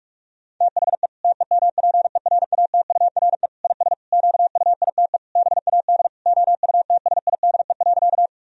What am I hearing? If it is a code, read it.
Morse code, heard as a keyed tone, about 41 wpm, 700 Hz. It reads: THE TEMPERATURE IS QUITE BAD OUTSIDE.
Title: Sentences from Top 1000 Words - Review Part 4 - 45wpm